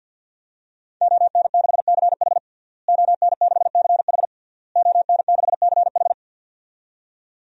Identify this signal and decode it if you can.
Morse code, heard as a keyed tone, about 52 wpm, 700 Hz. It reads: ON6CH ON6CH ON6CH
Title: DXCC-100 Call Signs - 5 Characters - Repeated 3x - 50wpm